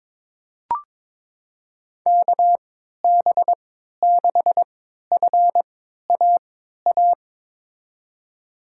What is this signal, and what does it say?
Morse code, heard as a keyed tone, about 21 wpm, spaced as for 12 wpm, 700 Hz. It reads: KB6FAA
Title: USA Call Signs - 6 Characters - 3x Character Spacing - 22wpm